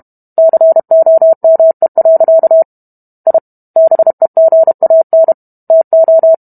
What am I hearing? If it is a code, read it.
Morse code, heard as a keyed tone, about 32 wpm, 650 Hz. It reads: COME. I BEGAN TO